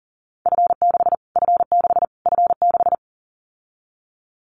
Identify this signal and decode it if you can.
Morse code, heard as a keyed tone, about 40 wpm, 700 Hz. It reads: F6 F6 F6
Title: Callsign Prefixes for Top 25 DXCC - Repeated 3x - 40wpm